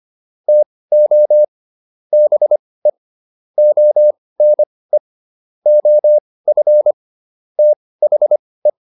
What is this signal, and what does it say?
Morse code, heard as a keyed tone, about 25 wpm, spaced as for 18 wpm, 600 Hz. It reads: TO BE ONE OF THE